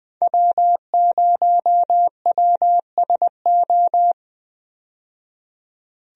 Morse code, heard as a keyed tone, about 20 wpm, 700 Hz. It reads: W0WSO
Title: USA Call Signs - 5 Characters - 20wpm